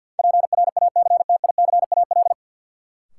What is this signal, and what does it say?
Morse code, heard as a keyed tone, about 52 wpm, 700 Hz. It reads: PRACTICAL
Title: Sentences from Top 1400 Words - New Word Drill - 50wpm